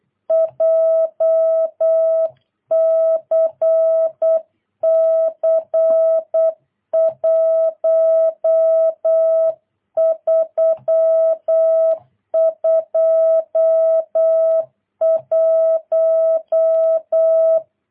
Morse code, heard as a keyed tone, about 8 wpm, 650 Hz. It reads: JCC 1321
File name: jh8cbh / 2022cw